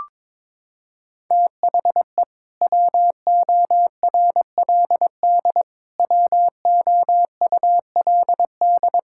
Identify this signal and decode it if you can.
Morse code, heard as a keyed tone, about 22 wpm, 700 Hz. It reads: THE WORLD WOULD
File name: Sentences with 7-Word Phrases - 22wpm